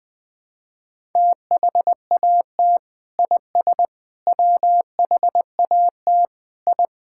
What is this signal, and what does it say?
Morse code, heard as a keyed tone, about 21 wpm, 700 Hz. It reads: THAT IS WHAT I
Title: Sentences with 2-Word Phrases - 20wpm